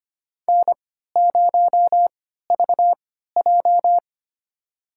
Morse code, heard as a keyed tone, about 25 wpm, spaced as for 14 wpm, 700 Hz. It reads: N0VJ
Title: USA Call Signs - 4 Characters - 3x Character Spacing - 25wpm